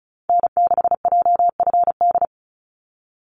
Morse code, heard as a keyed tone, about 34 wpm, 700 Hz. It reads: N6JFD
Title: NA SOTA Activator Callsigns 35wpm